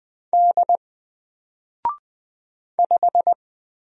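Morse code, heard as a keyed tone, about 20 wpm, 700 Hz. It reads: D 5
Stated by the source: Single letter and number Morse drill at 20 wpm, from N4LCD.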